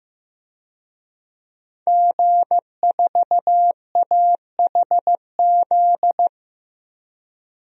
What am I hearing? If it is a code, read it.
Morse code, heard as a keyed tone, about 15 wpm, 700 Hz. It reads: G4AHZ
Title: DXCC-100 Call Signs - 5 Characters - 15wpm